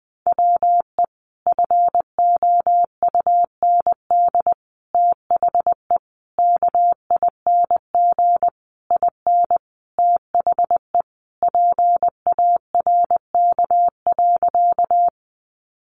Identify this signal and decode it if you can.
Morse code, heard as a keyed tone, about 19 wpm, 700 Hz. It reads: WE FOUND THE KING IN THE PARK.
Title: Sentences from Top 1300 Words 20wpm